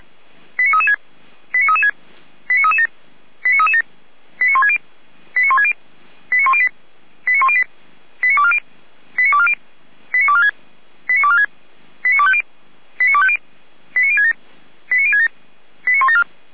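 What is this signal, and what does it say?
Five-tone selective call (ZVEI), heard as quick runs of five tones, tones of about 70 ms. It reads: 89336 89336 89337 89337 89160 89160 89108 89108 89240 89240 89246 89246 89250 89250 89067 89067 89174
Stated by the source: ZVEI five-tone call received on an SDR.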